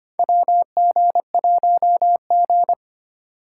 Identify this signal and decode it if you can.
Morse code, heard as a keyed tone, about 25 wpm, 700 Hz. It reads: WG1G
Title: USA Call Signs - 4 Characters - 8x Character Spacing - 25wpm